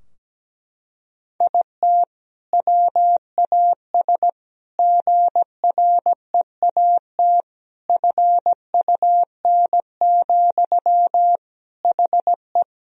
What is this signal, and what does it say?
Morse code, heard as a keyed tone, about 17 wpm, 700 Hz. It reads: IT WAS GREAT FUN, HE
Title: Sentences from Top 1000 Words - Review Part 4 - 17wpm